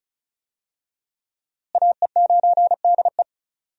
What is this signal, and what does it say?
Morse code, heard as a keyed tone, about 36 wpm, 700 Hz. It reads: AE9DE
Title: USA Call Signs - 5 Characters - Repeated 2x - 35wpm